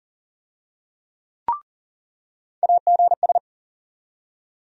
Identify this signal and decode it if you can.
Morse code, heard as a keyed tone, about 40 wpm, 700 Hz. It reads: AGS
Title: USA Call Sign Postfixes - 3 Letters 40wpm